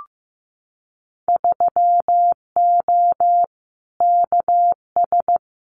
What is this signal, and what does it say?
Morse code, heard as a keyed tone, about 15 wpm, 700 Hz. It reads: 3O KS
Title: Winter Field Day Exchanges 15wpm